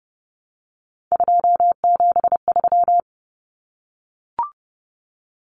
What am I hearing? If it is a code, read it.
Morse code, heard as a keyed tone, about 30 wpm, 700 Hz. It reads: 273
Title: Common Cut Numbers - 3-Digit 30wpm